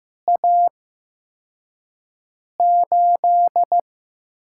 Morse code, heard as a keyed tone, about 15 wpm, spaced as for 4 wpm, 700 Hz. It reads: A8
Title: USA Call Signs - 5 Characters - 8x Character Spacing - 15wpm